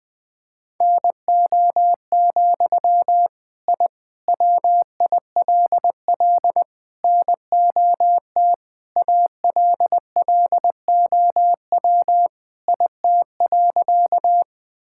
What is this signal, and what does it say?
Morse code, heard as a keyed tone, about 21 wpm, 700 Hz. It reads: NO, I WILL NOT ALLOW IT.
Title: Sentences from Top 1000 Words - Review Part 5 - 20wpm